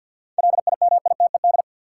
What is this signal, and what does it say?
Morse code, heard as a keyed tone, about 53 wpm, 700 Hz. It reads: LIMITED
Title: Sentences from Top 900 Words - New Word Drill - 50wpm